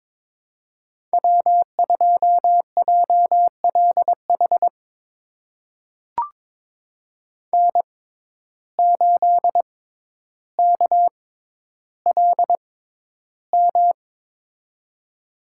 Morse code, heard as a keyed tone, about 21 wpm, spaced as for 11 wpm, 700 Hz. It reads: W2JLH N8KLM
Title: USA Call Signs - 5 Characters - 6x Character Spacing - 22wpm